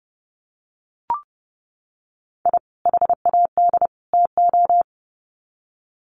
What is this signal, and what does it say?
Morse code, heard as a keyed tone, about 30 wpm, 700 Hz. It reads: I HAD TO
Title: Sets of 3 Words 30wpm